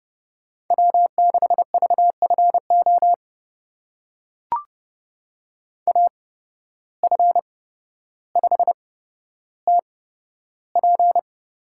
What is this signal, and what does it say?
Morse code, heard as a keyed tone, about 31 wpm, spaced as for 13 wpm, 700 Hz. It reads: W6VFO AF5TP
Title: USA Call Signs - 5 Characters - 8x Character Spacing - 30wpm